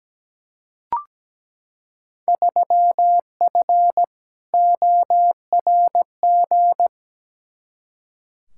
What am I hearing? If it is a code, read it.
Morse code, heard as a keyed tone, about 17 wpm, 700 Hz. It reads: 3F ORG
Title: Field Day Exchanges - 17wpm